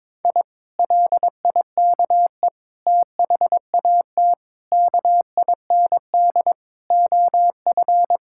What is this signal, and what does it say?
Morse code, heard as a keyed tone, about 22 wpm, 700 Hz. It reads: I LIKE THAT KIND OF
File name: practice / 22wpm / Sentences with 3-Word Phrases - 22wpm